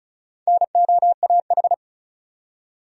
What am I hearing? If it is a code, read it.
Morse code, heard as a keyed tone, about 35 wpm, 700 Hz. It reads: NOAH
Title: USA Common Names - Max 4 Letters - 35wpm